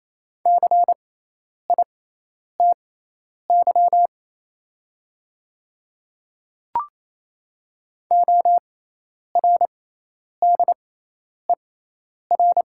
Morse code, heard as a keyed tone, about 28 wpm, spaced as for 10 wpm, 700 Hz. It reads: CITY ORDER
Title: 6x Character Spacing - Top 500 Words - 28wpm